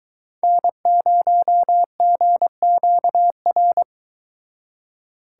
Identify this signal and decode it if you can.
Morse code, heard as a keyed tone, about 22 wpm, 700 Hz. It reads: N0GQR